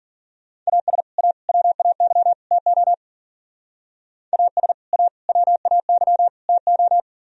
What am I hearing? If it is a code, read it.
Morse code, heard as a keyed tone, about 49 wpm, 700 Hz. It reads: AS A WAY TO AS A WAY TO
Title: Speed-Racing - Sets of 4 Words 40wpm